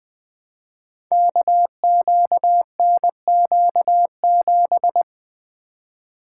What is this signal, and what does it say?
Morse code, heard as a keyed tone, about 20 wpm, 700 Hz. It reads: KQNQ7
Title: Code Groups - 5 Letter:Numbers - 20wpm